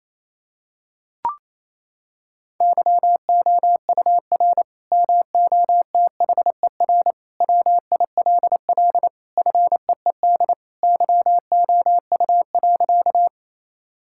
Morse code, heard as a keyed tone, about 28 wpm, 700 Hz. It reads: YOUR MOTHER WILL FEED YOU.
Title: Sentences from Top 1600 Words 28wpm